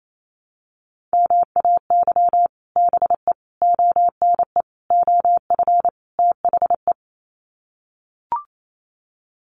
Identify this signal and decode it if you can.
Morse code, heard as a keyed tone, about 28 wpm, 700 Hz. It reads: MAY BE ONE OF THE
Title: Sets of 5 Words Encore2 28wpm